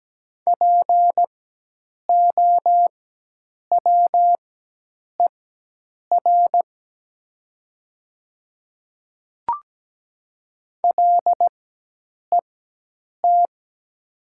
Morse code, heard as a keyed tone, about 17 wpm, spaced as for 8 wpm, 700 Hz. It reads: POWER LET
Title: 4x Character Spacing - Top 500 Words - 17wpm